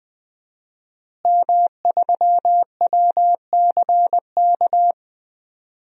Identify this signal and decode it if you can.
Morse code, heard as a keyed tone, about 20 wpm, 700 Hz. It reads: M3WCK
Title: DXCC-100 Call Signs - 5 Characters - Repeated 2x - 20wpm